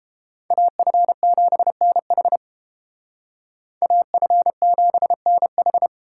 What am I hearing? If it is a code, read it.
Morse code, heard as a keyed tone, about 29 wpm, 700 Hz. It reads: AF7NH AF7NH